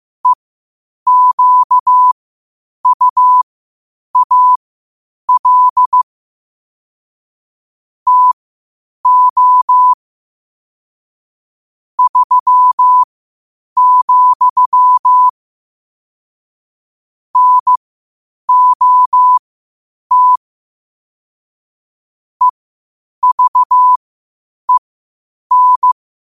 Morse code, heard as a keyed tone, about 15 wpm, spaced as for 8 wpm, 1000 Hz. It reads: EQUAL TO 3, NOT EVEN
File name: qotdfiles / 8wpm